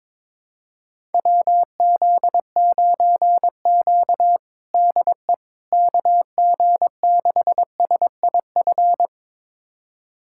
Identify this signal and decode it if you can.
Morse code, heard as a keyed tone, about 22 wpm, 700 Hz. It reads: WZ9Q DE KG6SIF
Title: USA Callsign DE Callsign 22wpm